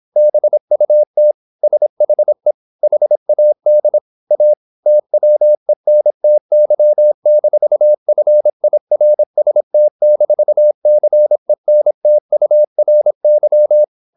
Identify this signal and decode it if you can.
Morse code, heard as a keyed tone, about 26 wpm, 600 Hz. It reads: BUT SHE HAD A TWENTY-FIRST-CENTURY
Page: Never 26wpm | CW med Gnister